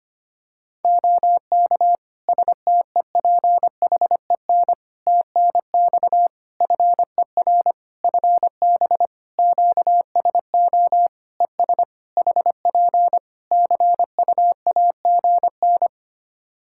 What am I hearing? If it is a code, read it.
Morse code, heard as a keyed tone, about 25 wpm, 700 Hz. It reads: OK STEPHEN TNX FER FB QSO ES HP CUAGN
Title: Practice QSO - Exchange 4 - Ending - 25wpm